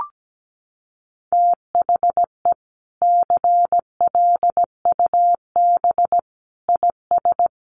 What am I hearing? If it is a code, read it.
Morse code, heard as a keyed tone, about 17 wpm, 700 Hz. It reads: THE CLUB IS
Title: Sentences from Top 1200 Words 17wpm